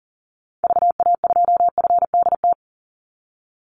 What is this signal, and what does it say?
Morse code, heard as a keyed tone, about 37 wpm, 700 Hz. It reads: VA2FDT
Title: Call Signs - 40wpm